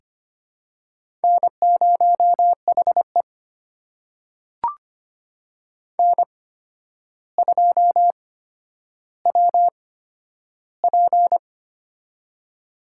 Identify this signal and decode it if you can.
Morse code, heard as a keyed tone, about 25 wpm, spaced as for 11 wpm, 700 Hz. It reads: N0HE N2WP
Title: USA Call Signs - 4 Characters - 8x Character Spacing - 25wpm